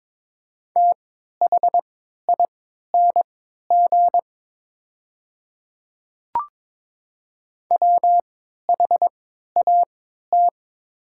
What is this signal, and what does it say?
Morse code, heard as a keyed tone, about 22 wpm, spaced as for 13 wpm, 690 Hz. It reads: THING WHAT
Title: 3x Character Spacing - Top 200 Words - 22wpm